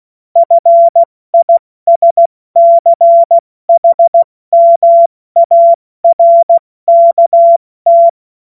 Morse code, heard as a keyed tone, about 16 wpm, 680 Hz. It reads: FISCHMARKT
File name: Morsekurs / Witze / Witz 08. August 2023 70 F80